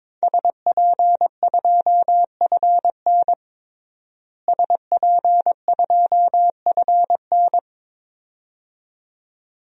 Morse code, heard as a keyed tone, about 22 wpm, 700 Hz. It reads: SP2FN SP2FN
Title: DXCC-100 Call Signs - 5 Characters - Repeated 2x - 22wpm